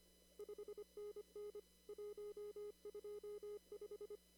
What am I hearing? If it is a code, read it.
Morse code, heard as a keyed tone, about 25 wpm, 420 Hz. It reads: 5NN 125
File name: CW / kt5j wpx cw-op-k5pi